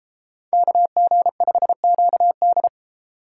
Morse code, heard as a keyed tone, about 33 wpm, 700 Hz. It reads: KG5QD